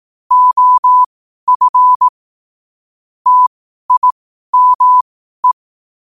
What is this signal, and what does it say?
Morse code, heard as a keyed tone, about 18 wpm, spaced as for 12 wpm, 1000 Hz. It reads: OF TIME
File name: qotdfiles / 12wpm